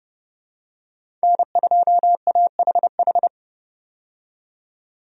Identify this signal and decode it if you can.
Morse code, heard as a keyed tone, about 30 wpm, 700 Hz. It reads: N2AHH
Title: USA Call Signs - 5 Characters - 30wpm